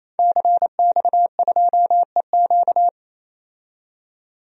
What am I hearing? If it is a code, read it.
Morse code, heard as a keyed tone, about 28 wpm, 700 Hz. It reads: CX2EQ
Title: DXCC-100 Call Signs - 5 Characters - 28wpm